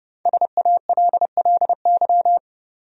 Morse code, heard as a keyed tone, about 32 wpm, 700 Hz. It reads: SALLY
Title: USA Common Names - 30wpm